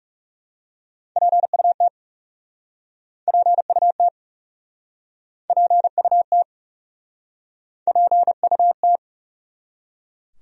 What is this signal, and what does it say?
Morse code, heard as a keyed tone, about 40 wpm, 700 Hz. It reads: PUT PUT PUT PUT